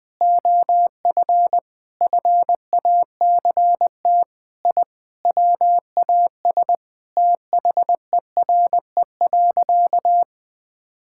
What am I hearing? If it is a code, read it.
Morse code, heard as a keyed tone, about 21 wpm, 700 Hz. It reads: OF FACT I WAS THERE.
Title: Sentences with 4-Word Phrases - 20wpm